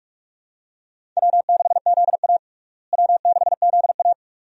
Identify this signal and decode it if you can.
Morse code, heard as a keyed tone, about 43 wpm, 700 Hz. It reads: W6ZA W6ZA
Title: USA Call Signs - 4 Characters - Repeated 2x - 45wpm